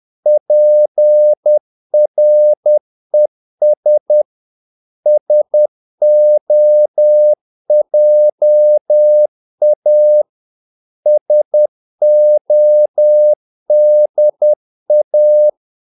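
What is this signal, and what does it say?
Morse code, heard as a keyed tone, about 10 wpm, 600 Hz. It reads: PRES SOJA SODA
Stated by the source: Morse practice of short Danish words at 10 wpm, from CW med Gnister.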